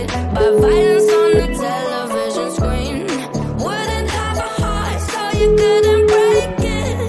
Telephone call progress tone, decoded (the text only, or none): ringback tone